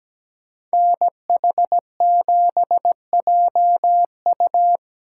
Morse code, heard as a keyed tone, about 20 wpm, 700 Hz. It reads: NH7JU
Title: Speed Racing - USA Call Signs 17wpm